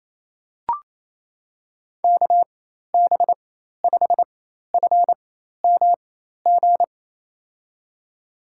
Morse code, heard as a keyed tone, about 28 wpm, spaced as for 13 wpm, 700 Hz. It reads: KB5FMG
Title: USA Call Signs - 6 Characters - 4x Character Spacing - 28wpm